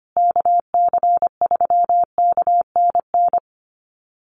Morse code, heard as a keyed tone, about 25 wpm, 700 Hz. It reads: KC3KNN
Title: USA Call Signs Repeated 3x 25wpm